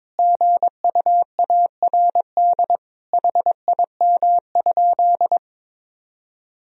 Morse code, heard as a keyed tone, about 22 wpm, 700 Hz. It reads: GUARD HIM?
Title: Sentences from Top 1000 Words - Review Part 4 - 22wpm